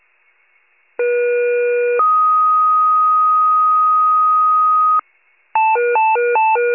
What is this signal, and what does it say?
Two-tone page, heard as a one-second tone, then a three-second tone, about 480 then 1250 Hz. It is heard from 1.0 to 5.0 s.